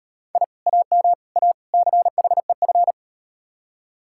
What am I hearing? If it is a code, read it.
Morse code, heard as a keyed tone, about 35 wpm, 700 Hz. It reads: I AM A CHEF